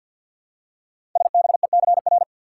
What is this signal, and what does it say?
Morse code, heard as a keyed tone, about 48 wpm, 690 Hz. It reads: SBECR
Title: Code Groups - 5 Letters - 50wpm